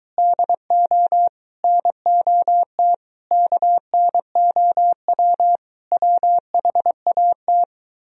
Morse code, heard as a keyed tone, about 23 wpm, 700 Hz. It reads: DO NOT KNOW WHAT